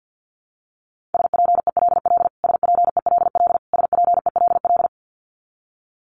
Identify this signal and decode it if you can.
Morse code, heard as a keyed tone, about 50 wpm, 700 Hz. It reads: SPELL SPELL SPELL
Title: Top 300 Words Repeated 50wpm